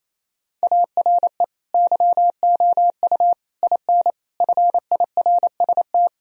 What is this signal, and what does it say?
Morse code, heard as a keyed tone, about 28 wpm, 700 Hz. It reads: ARE YOU IN FIRST
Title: Sentences from Top 1000 Words - Review Part 2 - 28wpm